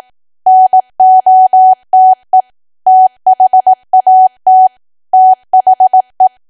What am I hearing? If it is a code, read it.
Morse code, heard as a keyed tone, about 18 wpm, 750 Hz. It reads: NOTE THAT THE
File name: cwa / W1AW-Level2 / 18WPM